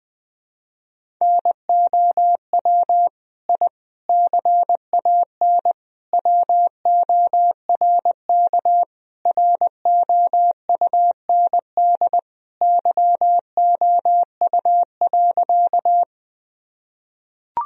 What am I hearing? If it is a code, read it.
Morse code, heard as a keyed tone, about 20 wpm, 700 Hz. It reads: NOW I CAN WORK ROUND YOU.
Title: Sentences from Top 1000 Words - Review Part 1 - 20wpm